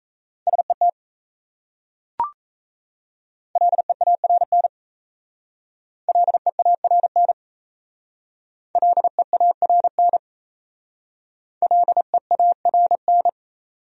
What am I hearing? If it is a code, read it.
Morse code, heard as a keyed tone, about 37 wpm, 700 Hz. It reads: SET LEARN LEARN LEARN LEARN